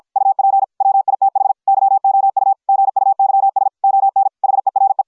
Morse code, heard as a keyed tone, about 53 wpm, 750 Hz. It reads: UP WITH YOU GUYS ON HERE